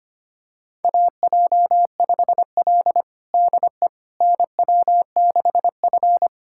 Morse code, heard as a keyed tone, about 25 wpm, 700 Hz. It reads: AJ5L DE NW6F
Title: USA Callsign DE Callsign - 4 Characters - 25wpm